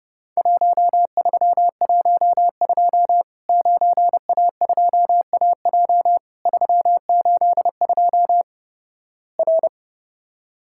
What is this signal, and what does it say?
Morse code, heard as a keyed tone, about 30 wpm, 700 Hz. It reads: 1312 9A2AJ 382 R
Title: Worked All Europe DX Contest - QTC Exchange - 30wpm